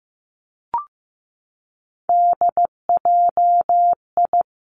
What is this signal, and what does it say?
Morse code, heard as a keyed tone, about 15 wpm, 700 Hz. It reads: DJI
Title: Three Challenging Letter-Number Combinations 15wpm